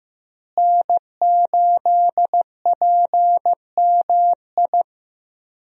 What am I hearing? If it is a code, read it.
Morse code, heard as a keyed tone, about 15 wpm, 700 Hz. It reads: N8PMI
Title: USA Call Signs - 5 Characters - Repeated 3x - 15wpm